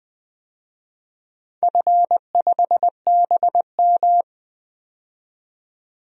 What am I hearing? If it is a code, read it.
Morse code, heard as a keyed tone, about 20 wpm, 700 Hz. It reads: F5BM
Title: DXCC-100 Call Signs - 4 Characters - 20wpm